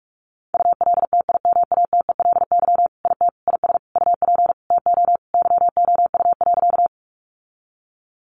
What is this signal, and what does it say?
Morse code, heard as a keyed tone, about 44 wpm, 700 Hz. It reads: ULTIMATELY IT IS UP TO YOU.
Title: Sentences from Top 1200 Words 45wpm